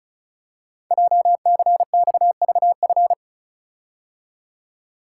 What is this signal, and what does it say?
Morse code, heard as a keyed tone, about 36 wpm, 700 Hz. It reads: JCXVF
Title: Code Groups - 5 Letters - 35wpm